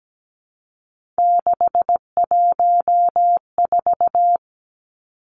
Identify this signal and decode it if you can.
Morse code, heard as a keyed tone, about 17 wpm, 700 Hz. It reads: 614
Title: Numbers - 3-Digit 17wpm